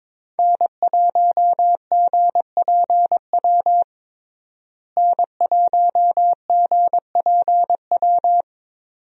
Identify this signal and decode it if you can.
Morse code, heard as a keyed tone, about 21 wpm, 700 Hz. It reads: N1GPW N1GPW
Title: USA Call Signs - 5 Characters - Repeated 2x - 22wpm